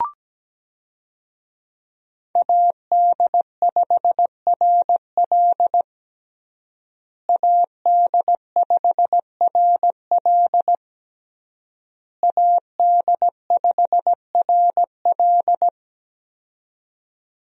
Morse code, heard as a keyed tone, about 17 wpm, 700 Hz. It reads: AD5RL AD5RL AD5RL
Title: USA Call Signs - 5 Characters - Repeated 3x - 17wpm